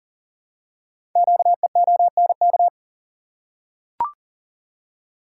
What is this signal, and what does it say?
Morse code, heard as a keyed tone, about 40 wpm, 700 Hz. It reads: QEONK
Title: Code Groups - 5 Letters - 40wpm